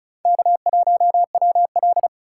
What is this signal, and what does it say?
Morse code, heard as a keyed tone, about 45 wpm, 700 Hz. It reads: K1WL